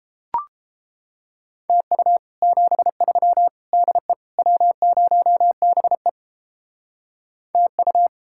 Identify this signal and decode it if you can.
Morse code, heard as a keyed tone, about 29 wpm, 700 Hz. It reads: TU 73 DE W0BE TU